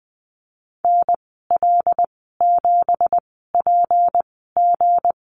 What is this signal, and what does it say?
Morse code, heard as a keyed tone, about 20 wpm, spaced as for 15 wpm, 700 Hz. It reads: NL7PG
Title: 173 Lesson 28g - Number 9 - Call Signs 20wpm